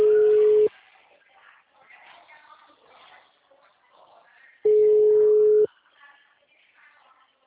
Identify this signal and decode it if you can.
Ringback tone (Europe, 425 Hz).